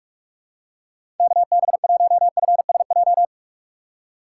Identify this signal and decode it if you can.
Morse code, heard as a keyed tone, about 44 wpm, 700 Hz. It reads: KB1FSJ
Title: USA Call Signs - 2x Character Spacing - 45wpm